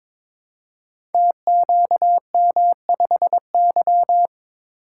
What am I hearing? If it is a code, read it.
Morse code, heard as a keyed tone, about 22 wpm, 700 Hz. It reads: TQM5Y
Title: Code Groups - 5 Letter:Numbers - 22wpm